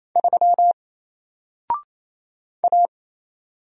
Morse code, heard as a keyed tone, about 28 wpm, 700 Hz. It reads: 3 A
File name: Single-Letter-Number-28wpm